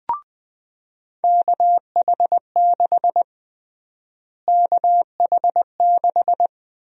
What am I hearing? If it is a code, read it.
Morse code, heard as a keyed tone, about 20 wpm, 700 Hz. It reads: KH6 KH6
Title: USA Call Sign Prefixes - Repeated 2x - 20wpm